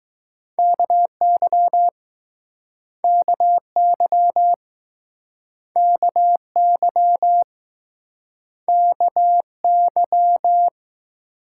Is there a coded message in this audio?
Yes: Morse code from 0.6 to 10.7 s, heard as a keyed tone, about 22 wpm, 700 Hz. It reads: KY KY KY KY